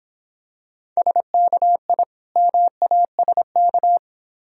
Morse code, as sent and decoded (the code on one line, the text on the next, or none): ... -.- .. / -- .- ... -.-
SKI MASK